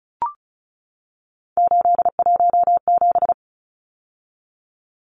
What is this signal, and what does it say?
Morse code, heard as a keyed tone, about 34 wpm, 700 Hz. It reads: 817
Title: Numbers - 3-Digit 35wpm